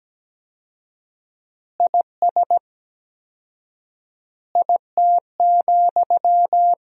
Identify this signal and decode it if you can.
Morse code, heard as a keyed tone, about 17 wpm, 700 Hz. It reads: IS IT,
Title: 4x Word-Spacing - Sentences from Top 200 Words - 17wpm